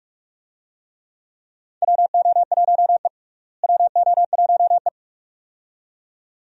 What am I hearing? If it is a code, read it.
Morse code, heard as a keyed tone, about 46 wpm, 700 Hz. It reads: WO1E WO1E
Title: USA Call Signs - 4 Characters - Repeated 2x - 45wpm